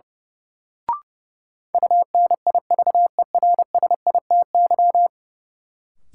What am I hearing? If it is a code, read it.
Morse code, heard as a keyed tone, about 32 wpm, 700 Hz. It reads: UNIVERSITY
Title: Sets of 4 Words - New Word Drill - 30wpm